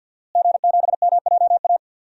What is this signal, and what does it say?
Morse code, heard as a keyed tone, about 49 wpm, 700 Hz. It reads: G7MJA
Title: DXCC-100 Call Signs - 5 Characters - Repeated 2x - 50wpm